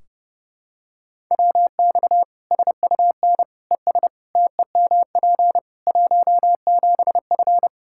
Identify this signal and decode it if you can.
Morse code, heard as a keyed tone, about 30 wpm, 700 Hz. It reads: WX SUN ES TEMP 17F
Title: K1USN SST QSO Practice 30wpm